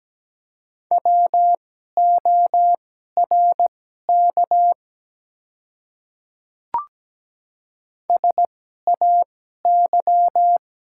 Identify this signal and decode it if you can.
Morse code, heard as a keyed tone, about 17 wpm, spaced as for 12 wpm, 700 Hz. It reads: WORK SAY
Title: 2x Character Spacing - Top 300 Words - 17wpm